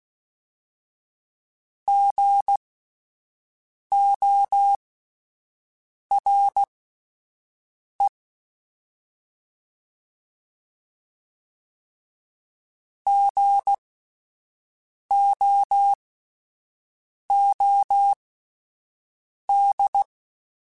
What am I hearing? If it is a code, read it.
Morse code, heard as a keyed tone, about 16 wpm, spaced as for 5 wpm, 800 Hz. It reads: GORE GOOD
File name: Spaced Alphas / Spaced Alphas mp3s / Gs sp @10